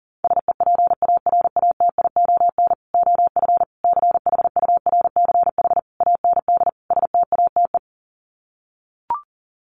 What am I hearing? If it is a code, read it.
Morse code, heard as a keyed tone, about 37 wpm, 700 Hz. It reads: SEPARATION OF CHURCH AND STATE
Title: Sets of 5 Words 40wpm